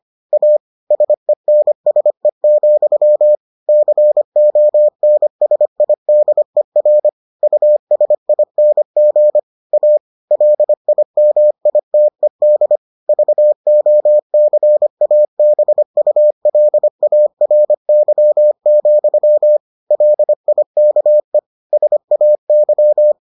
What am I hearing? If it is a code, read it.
Morse code, heard as a keyed tone, about 25 wpm, 600 Hz. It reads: A SENSE, CONSIDER USING A LIMITED VOCABULARY, LIKE SAY